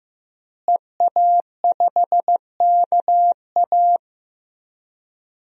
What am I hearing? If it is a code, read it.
Morse code, heard as a keyed tone, about 16 wpm, 700 Hz. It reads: EA5KA
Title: DXCC-100 Call Signs - 5 Characters - Repeated 3x - 15wpm